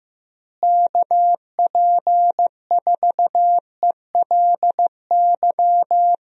Morse code, heard as a keyed tone, about 16 wpm, 700 Hz. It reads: KP4ELY
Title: USA Call Signs - 6 Characters - Repeated 3x - 15wpm